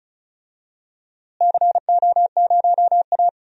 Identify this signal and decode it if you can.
Morse code, heard as a keyed tone, about 36 wpm, 700 Hz. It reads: CO0A
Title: DXCC-100 Call Signs - 4 Characters - Repeated 2x - 35wpm